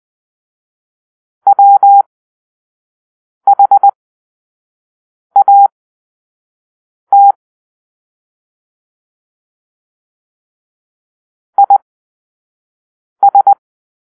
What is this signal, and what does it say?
Morse code, heard as a keyed tone, about 20 wpm, spaced as for 5 wpm, 800 Hz. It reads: WHAT IS